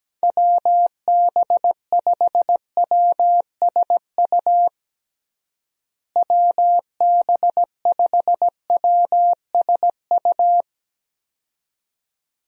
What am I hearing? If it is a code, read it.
Morse code, heard as a keyed tone, about 17 wpm, 700 Hz. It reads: WB5WSU WB5WSU
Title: USA Call Signs - 6 Characters - Repeated 2x - 17wpm